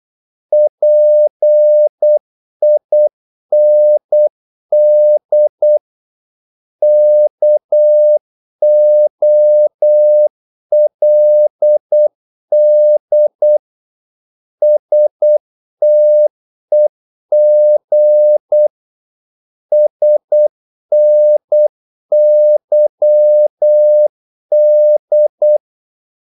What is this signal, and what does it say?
Morse code, heard as a keyed tone, about 8 wpm, 600 Hz. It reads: PIND KOLD STEG SNYD
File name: Korte ord DK 8wpm